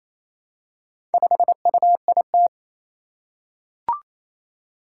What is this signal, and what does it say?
Morse code, heard as a keyed tone, about 27 wpm, 700 Hz. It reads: 5UIT
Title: Four Challenging Letter-Number Combinations 28wpm